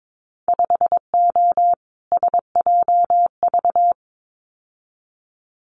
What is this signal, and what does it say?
Morse code, heard as a keyed tone, about 22 wpm, 700 Hz. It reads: 5O SJV
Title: Winter Field Day Exchanges Encore 22wpm